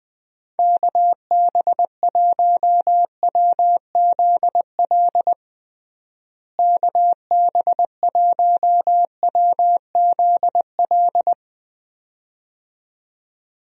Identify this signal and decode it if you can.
Morse code, heard as a keyed tone, about 20 wpm, 700 Hz. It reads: KB1WZL KB1WZL
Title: USA Call Signs - 6 Characters - Repeated 2x - 20wpm